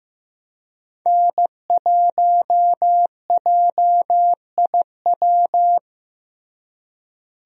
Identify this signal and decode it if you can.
Morse code, heard as a keyed tone, about 15 wpm, 700 Hz. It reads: N1JIW
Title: USA Call Signs - 5 Characters - Repeated 3x - 15wpm